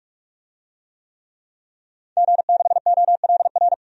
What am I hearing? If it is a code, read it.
Morse code, heard as a keyed tone, about 41 wpm, 700 Hz. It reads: G6OLR
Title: DXCC-100 Call Signs - 5 Characters - 45wpm